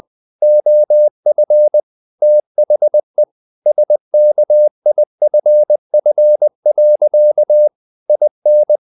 Morse code, heard as a keyed tone, about 20 wpm, 600 Hz. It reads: OF THE SKIFF. IN
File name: the old man and the sea / the old man and the sea part 1 20wpm